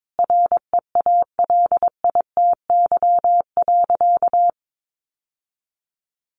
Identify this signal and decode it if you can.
Morse code, heard as a keyed tone, about 22 wpm, 700 Hz. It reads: REALITY.
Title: Sentences from Top 1200 Words 22wpm